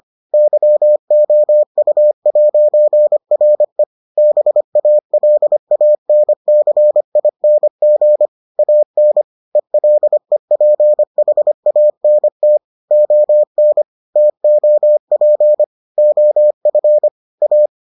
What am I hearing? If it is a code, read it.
Morse code, heard as a keyed tone, about 25 wpm, 600 Hz. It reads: YOU'RE BALANCING AN ELEPHANT ON TOP OF A